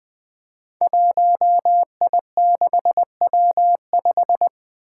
Morse code, heard as a keyed tone, about 20 wpm, 700 Hz. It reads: 1I6W5
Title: Five Challenging Letter-Number Combinations 20wpm